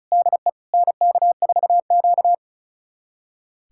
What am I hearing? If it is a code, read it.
Morse code, heard as a keyed tone, about 34 wpm, 700 Hz. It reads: DE NK4Q
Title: USA Callsign DE Callsign - 4 Characters - 35wpm